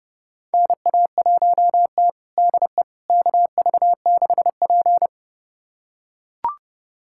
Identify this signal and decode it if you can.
Morse code, heard as a keyed tone, about 30 wpm, 700 Hz. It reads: NA1T DE KV6P
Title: USA Callsign DE Callsign - 4 Characters - 30wpm